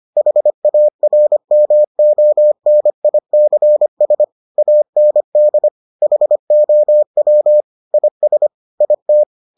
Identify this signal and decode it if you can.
Morse code, heard as a keyed tone, about 25 wpm, 600 Hz. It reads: HARMONICS AND HOW IS IT